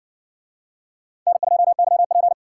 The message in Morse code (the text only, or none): N2YP